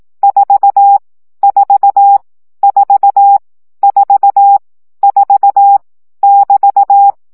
Morse code, heard as a keyed tone, about 18 wpm, 800 Hz. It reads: 4 4 4 4 4 <BT>
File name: CW gyakorlas 4 15 baud